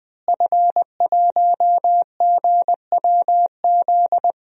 Morse code, heard as a keyed tone, about 20 wpm, 700 Hz. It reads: F1GWZ
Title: DXCC-100 Call Signs - 5 Characters - Repeated 2x - 20wpm